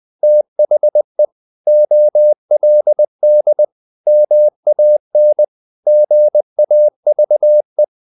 Morse code, heard as a keyed tone, about 20 wpm, 600 Hz. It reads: THE OLD MAN GAVE
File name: the old man and the sea part 2 20wpm